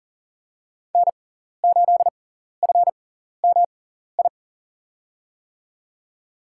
Morse code, heard as a keyed tone, about 40 wpm, spaced as for 14 wpm, 700 Hz. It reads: N8FMI
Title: USA Call Signs - 5 Characters - 6x Character Spacing - 40wpm